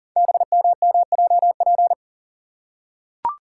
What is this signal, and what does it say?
Morse code, heard as a keyed tone, about 42 wpm, 690 Hz. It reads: BMMJP
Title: Code Groups - 5 Letter:Numbers - 40wpm